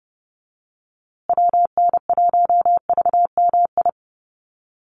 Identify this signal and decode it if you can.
Morse code, heard as a keyed tone, about 30 wpm, 700 Hz. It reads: WN1VMI
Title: USA Call Signs Repeated 3x 30wpm